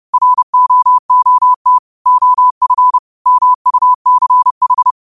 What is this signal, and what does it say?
Morse code, heard as a keyed tone, about 30 wpm, 1000 Hz. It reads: ROOT OF MUCH
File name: qotdfiles / 30wpm